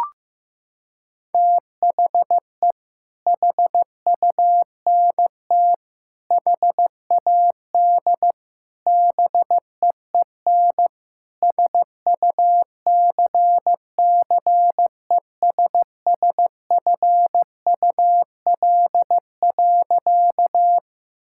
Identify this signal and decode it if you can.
Morse code, heard as a keyed tone, about 15 wpm, 700 Hz. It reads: THE HUNT HAD BEEN SUCCESSFUL.
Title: Sentences from Top 1600 Words 15wpm